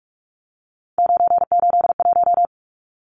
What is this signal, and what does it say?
Morse code, heard as a keyed tone, about 44 wpm, 700 Hz. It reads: 981
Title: Numbers - 3-Digit 45wpm